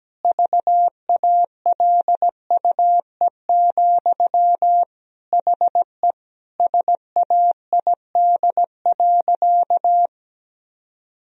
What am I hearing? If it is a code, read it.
Morse code, heard as a keyed tone, about 17 wpm, 700 Hz. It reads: VALUE, HE SAID.
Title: Sentences from Top 1000 Words - Review Part 3 - 17wpm